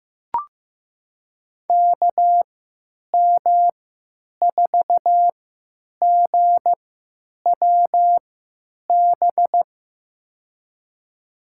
Morse code, heard as a keyed tone, about 15 wpm, spaced as for 9 wpm, 700 Hz. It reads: KM4GWB
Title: USA Call Signs - 6 Characters - 3x Character Spacing - 15wpm